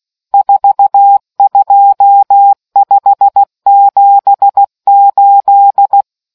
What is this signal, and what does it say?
Morse code, heard as a keyed tone, about 16 wpm, 800 Hz. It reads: 42578